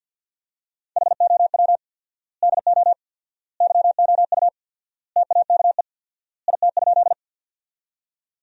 Morse code, heard as a keyed tone, about 50 wpm, 700 Hz. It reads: HOW DO YOU TAKE IT?
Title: 4x Word-Spacing - Sentences from Top 200 Words - 50wpm